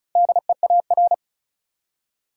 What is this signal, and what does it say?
Morse code, heard as a keyed tone, about 36 wpm, 700 Hz. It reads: DEAR
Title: Four Letter Words - 35wpm